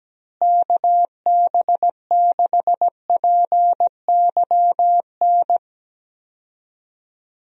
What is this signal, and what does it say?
Morse code, heard as a keyed tone, about 17 wpm, 700 Hz. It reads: KB6PYN
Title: USA Call Signs - 6 Characters - Repeated 2x - 17wpm